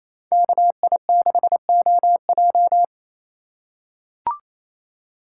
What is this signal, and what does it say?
Morse code, heard as a keyed tone, about 28 wpm, 700 Hz. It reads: KI6OJ
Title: USA Call Signs - 5 Characters - 4x Character Spacing - 28wpm